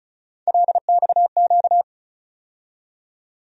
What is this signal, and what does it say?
Morse code, heard as a keyed tone, about 34 wpm, 700 Hz. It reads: LXQ
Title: USA Call Sign Postfixes - 3 Letters 35wpm